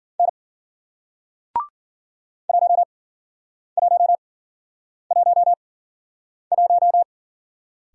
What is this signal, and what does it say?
Morse code, heard as a keyed tone, about 51 wpm, spaced as for 9 wpm, 700 Hz. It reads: N 1111